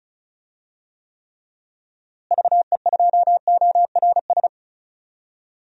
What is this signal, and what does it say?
Morse code, heard as a keyed tone, about 34 wpm, 700 Hz. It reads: VE2ORS
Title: DXCC-100 Call Signs - 6 Characters - Repeated 3x - 35wpm